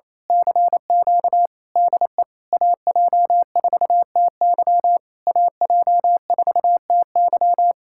Morse code, heard as a keyed tone, about 26 wpm, 700 Hz. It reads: CQ DE AJ4TY AJ4TY
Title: Speed-Racing Common QSO Phrases Encore - 28wpm